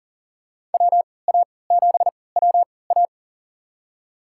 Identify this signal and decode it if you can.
Morse code, heard as a keyed tone, about 40 wpm, spaced as for 23 wpm, 700 Hz. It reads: WA7WA
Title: USA Call Signs - 5 Characters - 3x Character Spacing - 40wpm